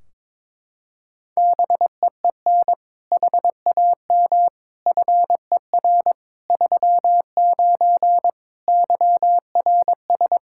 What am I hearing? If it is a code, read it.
Morse code, heard as a keyed tone, about 22 wpm, 700 Hz. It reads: BEEN HAM FER 39 YRS
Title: K1USN SST QSO Practice 22wpm